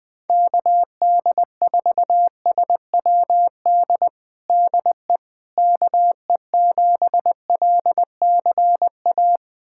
Morse code, heard as a keyed tone, about 21 wpm, 700 Hz. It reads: KD4SWD DE KE7LCA
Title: USA Callsign DE Callsign - 6 Characters - 20wpm